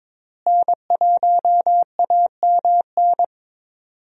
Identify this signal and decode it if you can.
Morse code, heard as a keyed tone, about 21 wpm, 700 Hz. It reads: N1AMN
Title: USA Call Signs - 5 Characters - Repeated 3x - 22wpm